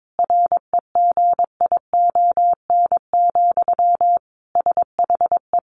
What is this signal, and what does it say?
Morse code, heard as a keyed tone, about 22 wpm, 700 Hz. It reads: REGION, SHE
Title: Sentences from Top 1500 Words 22wpm